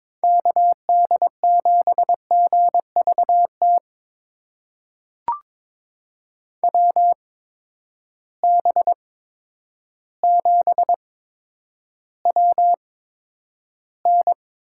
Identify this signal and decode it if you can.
Morse code, heard as a keyed tone, about 22 wpm, spaced as for 10 wpm, 700 Hz. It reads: KD7GVT WB7WN
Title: USA Call Signs - 6 Characters - 8x Character Spacing - 22wpm